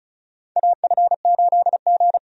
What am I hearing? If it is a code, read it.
Morse code, heard as a keyed tone, about 34 wpm, 700 Hz. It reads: AF8G
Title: USA Call Signs - 4 Characters - 35wpm